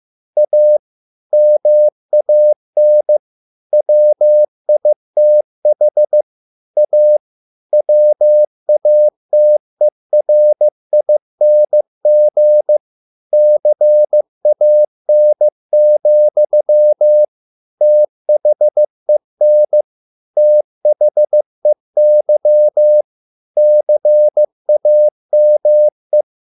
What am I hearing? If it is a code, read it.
Morse code, heard as a keyed tone, about 15 wpm, 600 Hz. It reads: A MAN WITH A WATERING CAN, THEN THEY CAME